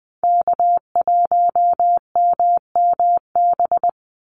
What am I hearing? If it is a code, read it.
Morse code, heard as a keyed tone, about 20 wpm, 700 Hz. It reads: K1MMB
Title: Call Signs - 20wpm